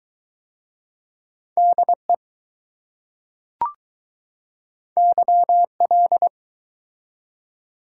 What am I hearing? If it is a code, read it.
Morse code, heard as a keyed tone, about 20 wpm, 700 Hz. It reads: DE YL